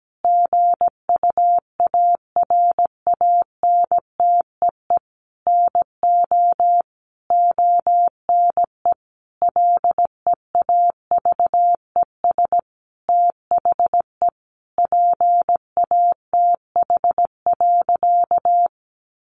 Morse code, heard as a keyed tone, about 17 wpm, 700 Hz. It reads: GUARANTEE NO ONE LEAVES THE PATH.
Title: A Sound Of Thunder 17wpm